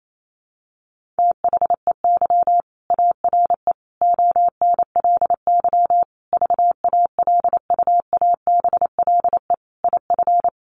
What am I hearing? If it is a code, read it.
Morse code, heard as a keyed tone, about 28 wpm, 700 Hz. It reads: THEY ARE ONLY VALUABLE IF
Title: Sentences from Top 1400 Words 28wpm